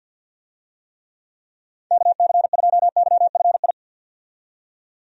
Morse code, heard as a keyed tone, about 52 wpm, 700 Hz. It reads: KC2YFI
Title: USA Call Signs - 6 Characters - 50wpm